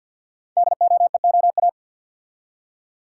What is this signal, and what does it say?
Morse code, heard as a keyed tone, about 48 wpm, 690 Hz. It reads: DOEOA
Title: Code Groups - 5 Letters - 50wpm